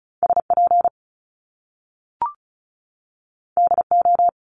Morse code, heard as a keyed tone, about 36 wpm, 700 Hz. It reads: SP DO
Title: Top Two Letter Combinations in Words 35wpm